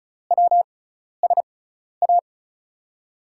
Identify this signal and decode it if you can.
Morse code, heard as a keyed tone, about 36 wpm, spaced as for 12 wpm, 690 Hz. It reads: WSA
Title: USA Call Signs - 6 Characters - 6x Character Spacing - 35wpm